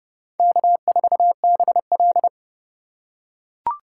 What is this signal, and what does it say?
Morse code, heard as a keyed tone, about 29 wpm, 700 Hz. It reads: K4BL